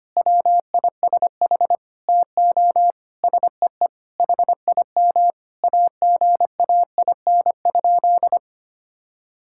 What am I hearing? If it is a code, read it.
Morse code, heard as a keyed tone, about 25 wpm, 700 Hz. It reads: WISH TO SEE HIM AGAIN?
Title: Sentences from Top 1600 Words 25wpm